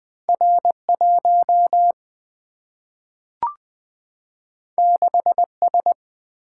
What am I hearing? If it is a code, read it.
Morse code, heard as a keyed tone, about 20 wpm, 700 Hz. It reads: R 1 6 S
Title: Single-Letter Single-Number 20wpm